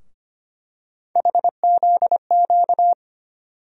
Morse code, heard as a keyed tone, about 25 wpm, 700 Hz. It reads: HZQ
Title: USA Call Sign Postfixes - 3 Letters 25wpm